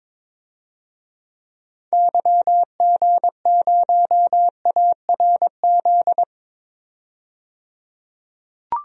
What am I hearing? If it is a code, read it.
Morse code, heard as a keyed tone, about 22 wpm, 700 Hz. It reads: YG0ARZ
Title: DXCC-100 Call Signs - 6 Characters - 22wpm